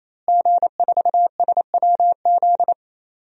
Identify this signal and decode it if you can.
Morse code, heard as a keyed tone, about 28 wpm, 700 Hz. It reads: G4SWZ
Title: DXCC-100 Call Signs - 5 Characters - Repeated 3x - 28wpm